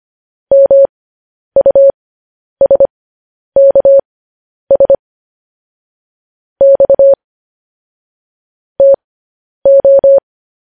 Morse code, heard as a keyed tone, about 25 wpm, spaced as for 10 wpm, 550 Hz. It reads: MUSKS X TO
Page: news headlines in morse code at 10 wpm